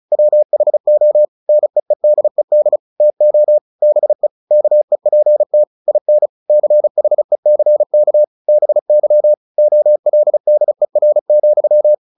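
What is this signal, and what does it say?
Morse code, heard as a keyed tone, about 36 wpm, 600 Hz. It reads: WHO NEEDED TO BE KEPT IN CHECK BY OLDER,
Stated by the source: Morse practice at 35 wpm from CW med Gnister, sending Ken Follett's novel Never in English.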